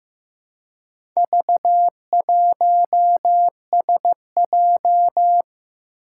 Morse code, heard as a keyed tone, about 15 wpm, 700 Hz. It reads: V1SJ
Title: Four Challenging Letter-Number Combinations 15wpm